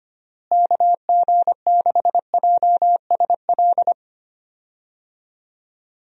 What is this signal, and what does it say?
Morse code, heard as a keyed tone, about 25 wpm, 700 Hz. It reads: KG6JSL
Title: USA Call Signs - 6 Characters - Repeated 3x - 25wpm